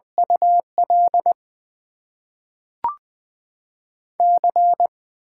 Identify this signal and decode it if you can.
Morse code, heard as a keyed tone, about 20 wpm, 700 Hz. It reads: UL C
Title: Top 90 QSO Elements - 6x Character Spacing - 20wpm